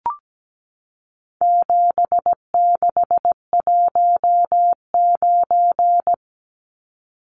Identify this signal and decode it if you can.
Morse code, heard as a keyed tone, about 17 wpm, 700 Hz. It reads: 7619
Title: Numbers - 4-Digit 17wpm